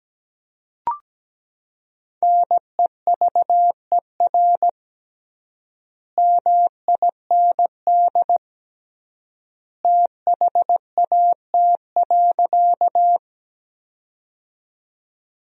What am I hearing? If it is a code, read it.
Morse code, heard as a keyed tone, about 17 wpm, 700 Hz. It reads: NEVER MIND THAT.
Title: 3x Word-Spacing - Sentences from Top 500 Words - 17wpm